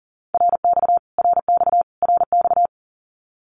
Morse code, heard as a keyed tone, about 40 wpm, 700 Hz. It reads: RX RX RX
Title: Abbreviations Repeated 3x - 40wpm